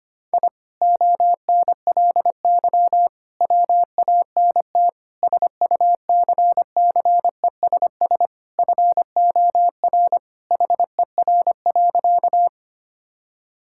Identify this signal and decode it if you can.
Morse code, heard as a keyed tone, about 25 wpm, 700 Hz. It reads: I ONLY WANT SUCCESS FOR HER.
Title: Sentences from Top 1000 Words - Review Part 6 - 25wpm